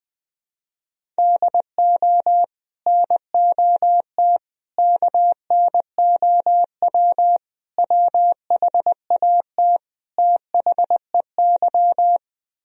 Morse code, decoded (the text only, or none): DO NOT KNOW WHAT THEY